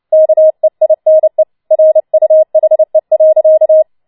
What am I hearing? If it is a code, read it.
Morse code, heard as a keyed tone, about 29 wpm, 630 Hz. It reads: KEINE RUHE.
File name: Schneewittchen Teil3 30wpm